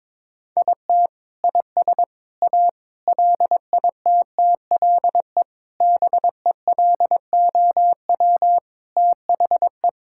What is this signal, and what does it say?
Morse code, heard as a keyed tone, about 22 wpm, 700 Hz. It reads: IT IS A LITTLE BELOW THE
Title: Sentences from Top 1000 Words - Review Part 5 - 22wpm